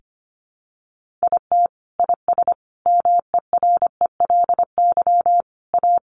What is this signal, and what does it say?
Morse code, heard as a keyed tone, about 25 wpm, 700 Hz. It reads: IT IS MERELY A
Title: Sentences from Top 1400 Words 25wpm